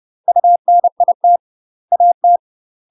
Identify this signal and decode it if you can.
Morse code, heard as a keyed tone, about 30 wpm, 700 Hz. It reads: UNIT AT
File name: UK MASTER QSOs TRAINING / 30WPM / UK MASTER QSOs 0018